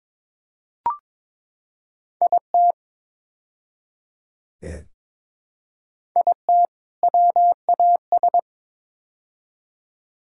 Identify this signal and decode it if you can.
Morse code, heard as a keyed tone, about 22 wpm, 700 Hz. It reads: IT IT WAS
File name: practice / 22wpm / Build A Phrase - 5-Word Phrases - 22wpm